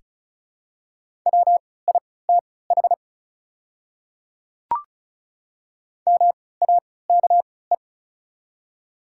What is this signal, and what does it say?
Morse code, heard as a keyed tone, about 36 wpm, spaced as for 20 wpm, 690 Hz. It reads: WITH MAKE
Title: 3x Character Spacing - Top 100 Words - 35wpm